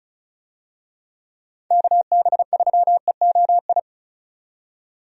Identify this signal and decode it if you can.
Morse code, heard as a keyed tone, about 34 wpm, 700 Hz. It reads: KB3EOI
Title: USA Call Signs - 6 Characters - 35wpm